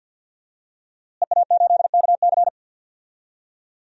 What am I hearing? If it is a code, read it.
Morse code, heard as a keyed tone, about 48 wpm, 700 Hz. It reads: EA8KC
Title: DXCC-100 Call Signs - 5 Characters - Repeated 2x - 50wpm